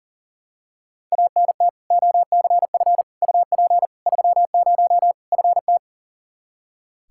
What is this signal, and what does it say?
Morse code, heard as a keyed tone, about 37 wpm, 700 Hz. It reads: ANT OCF UP 30 FT
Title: Practice QSO - Exchange 2 - Info - 40wpm